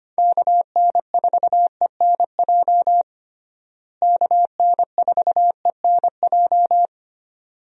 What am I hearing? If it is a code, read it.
Morse code, heard as a keyed tone, about 25 wpm, 700 Hz. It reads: KN4ENJ KN4ENJ
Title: USA Call Signs - 6 Characters - Repeated 2x - 25wpm